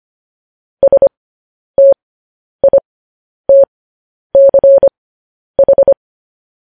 Morse code, heard as a keyed tone, about 25 wpm, spaced as for 10 wpm, 550 Hz. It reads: STITCH
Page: news-headlines-in-morse-code-at-10-wpm